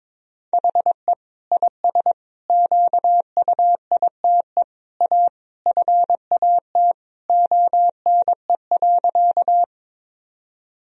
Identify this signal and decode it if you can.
Morse code, heard as a keyed tone, about 21 wpm, 700 Hz. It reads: HE IS QUITE A FAT ONE.
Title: Sentences from Top 1000 Words - Review Part 4 - 22wpm